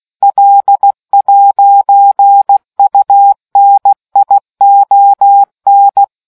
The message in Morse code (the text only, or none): L'UNION